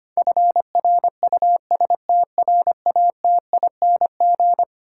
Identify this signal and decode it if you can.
Morse code, heard as a keyed tone, about 25 wpm, 700 Hz. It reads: FRUSTRATING
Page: morse-code-ninja-practice-25wpm